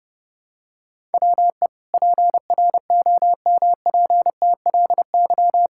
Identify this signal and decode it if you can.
Morse code, heard as a keyed tone, about 30 wpm, 700 Hz. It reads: WE PROMPTLY
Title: Pangram Challenge v2 - 30wpm